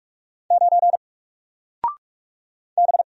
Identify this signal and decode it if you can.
Morse code, heard as a keyed tone, about 43 wpm, 700 Hz. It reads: 9 B
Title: Single Character (Letter:Number:Punctuation:Prosign) Alt 45wpm